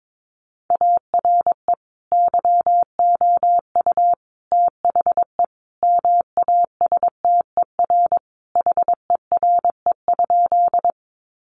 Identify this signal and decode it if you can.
Morse code, heard as a keyed tone, about 22 wpm, 700 Hz. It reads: ARE YOU THE MASTER HERE?
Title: Sentences from Top 1300 Words 22wpm